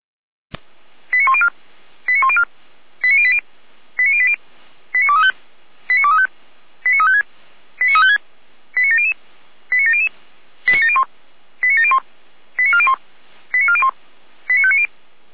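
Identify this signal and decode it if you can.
Five-tone selective call (ZVEI), heard as quick runs of five tones, tones of about 70 ms. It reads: 89104 89104 89080 89080 89235 89235 89356 89356 89700 89700 89701 89701 89401 89401 89590
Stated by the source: ZVEI five-tone selective call 89358 received on an SDR.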